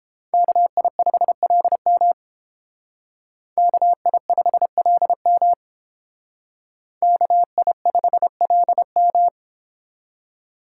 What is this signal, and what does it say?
Morse code, heard as a keyed tone, about 30 wpm, 700 Hz. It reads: KI5LM KI5LM KI5LM